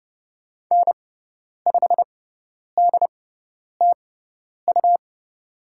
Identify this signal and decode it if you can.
Morse code, heard as a keyed tone, about 30 wpm, spaced as for 10 wpm, 700 Hz. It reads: N5DTU
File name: Lesson 17g - Letter u - Call Signs 30wpm